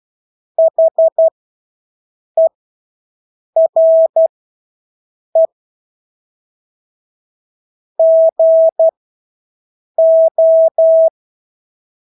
Morse code, heard as a keyed tone, about 12 wpm, spaced as for 6 wpm, 650 Hz. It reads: HERE GO
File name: Engelske ord 12-6 wpm